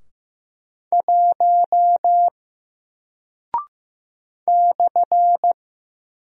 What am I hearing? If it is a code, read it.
Morse code, heard as a keyed tone, about 15 wpm, 700 Hz. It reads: 1 /
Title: Single Character (Letter:Number:Punctuation:Prosign) Alt 15wpm